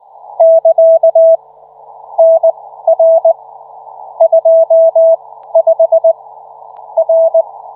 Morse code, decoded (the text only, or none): <KA> NR 25 R